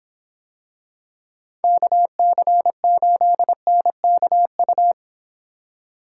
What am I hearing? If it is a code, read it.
Morse code, heard as a keyed tone, about 26 wpm, 700 Hz. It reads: KC8NKU